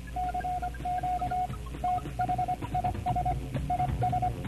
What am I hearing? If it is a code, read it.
Morse code, heard as a keyed tone, about 25 wpm, 720 Hz, with radio noise. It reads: CQ THIS IS